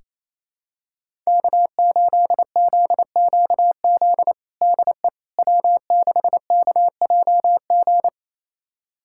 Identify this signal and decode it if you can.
Morse code, heard as a keyed tone, about 27 wpm, 700 Hz. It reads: K8ZQZ DE W6KJG
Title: USA Callsign DE Callsign - 5 Characters - 28wpm